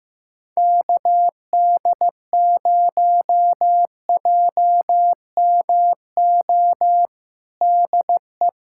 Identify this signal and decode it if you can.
Morse code, heard as a keyed tone, about 15 wpm, 700 Hz. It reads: KD0JMO DE
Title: USA Callsign DE Callsign - 6 Characters - 15wpm